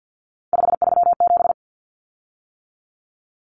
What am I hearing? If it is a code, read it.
Morse code, heard as a keyed tone, about 50 wpm, 700 Hz. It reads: 537
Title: Common Cut Numbers - 3-Digit 50wpm